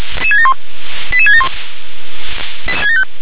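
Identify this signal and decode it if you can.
Five-tone selective call (ZVEI), heard as quick runs of five tones, tones of about 70 ms. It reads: 88671
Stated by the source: Five-tone ZVEI selective call received on SDR.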